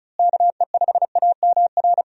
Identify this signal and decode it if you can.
Morse code, heard as a keyed tone, about 36 wpm, 700 Hz. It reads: KE5AMR
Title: USA Call Signs - 6 Characters - Repeated 3x - 35wpm